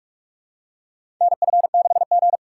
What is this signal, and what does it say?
Morse code, heard as a keyed tone, about 45 wpm, 700 Hz. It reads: NF6G
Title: USA Call Signs - 4 Characters - 6x Character Spacing - 45wpm